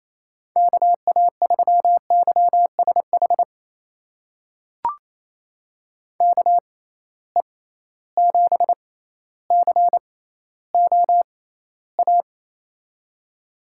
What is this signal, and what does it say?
Morse code, heard as a keyed tone, about 28 wpm, spaced as for 14 wpm, 700 Hz. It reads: KA3YSH KE7COA
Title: USA Call Signs - 6 Characters - 6x Character Spacing - 28wpm